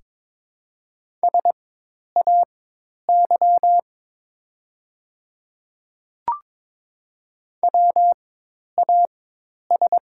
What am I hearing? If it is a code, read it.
Morse code, heard as a keyed tone, about 22 wpm, spaced as for 10 wpm, 700 Hz. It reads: SAY WAS
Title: 4x Character Spacing - Top 200 Words - 22wpm